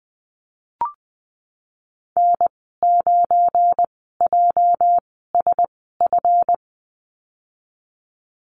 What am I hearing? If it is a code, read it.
Morse code, heard as a keyed tone, about 20 wpm, spaced as for 15 wpm, 700 Hz. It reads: N9JSF
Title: 231 Lesson 37g - Letter x - Call Signs 20wpm